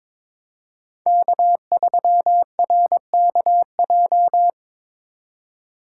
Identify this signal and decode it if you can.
Morse code, heard as a keyed tone, about 22 wpm, 700 Hz. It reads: K3RKJ
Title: USA Call Signs - 5 Characters - 3x Character Spacing - 22wpm